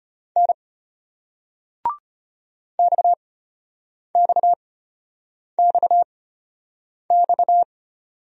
Morse code, heard as a keyed tone, about 35 wpm, spaced as for 8 wpm, 700 Hz. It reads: N XXXX